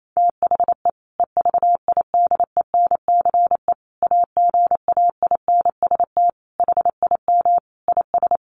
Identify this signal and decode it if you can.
Morse code, heard as a keyed tone, about 28 wpm, 700 Hz. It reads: THE EVIDENCE AGAINST HIM IS
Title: Sentences from Top 1300 Words 28wpm